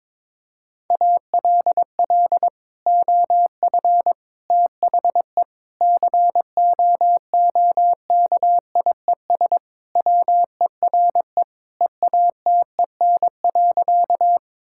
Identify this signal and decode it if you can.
Morse code, heard as a keyed tone, about 22 wpm, 700 Hz. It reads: ALL OF THE COOKIES WERE EATEN.
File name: practice / 22wpm / Sentences with 2-Word Phrases - 22wpm